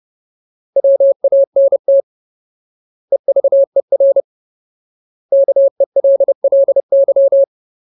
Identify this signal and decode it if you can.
Morse code, heard as a keyed tone, about 30 wpm, 550 Hz. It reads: WANT EVER KELLY
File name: Ultimate Word List / 30WPM Ultimate